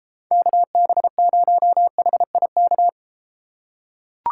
Morse code, heard as a keyed tone, about 32 wpm, 700 Hz. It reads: KB0HIK